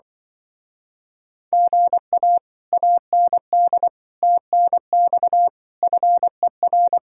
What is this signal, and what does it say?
Morse code, heard as a keyed tone, about 24 wpm, 700 Hz. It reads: GA AND TNX FER
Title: Speed-Racing Common QSO Phrases - 20wpm